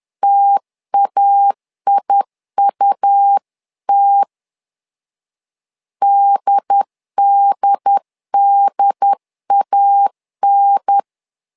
Morse code, heard as a keyed tone, about 10 wpm, 800 Hz. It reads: TAIUT DDDAN
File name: LEZIONI / 2 UHDN / 2 UHDN 10wpm 1904